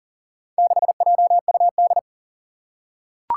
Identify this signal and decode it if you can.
Morse code, heard as a keyed tone, about 40 wpm, 700 Hz. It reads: 6JUD
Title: Four Challenging Letter-Number Combinations 40wpm